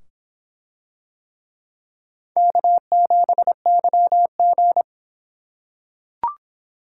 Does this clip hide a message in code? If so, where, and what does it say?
Yes: Morse code from 2.4 to 4.8 s, heard as a keyed tone, about 26 wpm, 700 Hz. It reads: K7YG